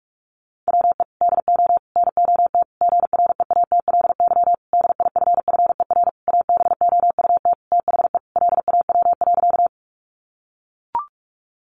Morse code, heard as a keyed tone, about 44 wpm, 700 Hz. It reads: WE DO NOT GREATLY DIFFER ABOUT THE LAW.
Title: Sentences from Top 1400 Words 45wpm